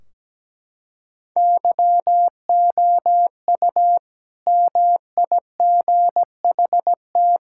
Morse code, heard as a keyed tone, about 17 wpm, 700 Hz. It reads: YOU MIGHT
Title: Sentences from Top 1600 Words 17wpm